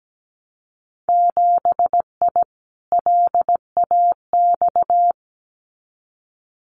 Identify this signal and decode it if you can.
Morse code, heard as a keyed tone, about 17 wpm, 700 Hz. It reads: 7I LAX
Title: Winter Field Day Exchanges Encore 17wpm